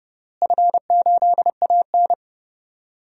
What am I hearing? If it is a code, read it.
Morse code, heard as a keyed tone, about 32 wpm, 700 Hz. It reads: F8AN
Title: DXCC-100 Call Signs - 4 Characters - Repeated 2x - 30wpm